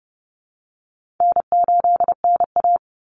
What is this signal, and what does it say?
Morse code, heard as a keyed tone, about 30 wpm, 700 Hz. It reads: N8NA
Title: Easy Call Signs Repeated 3x - 30wpm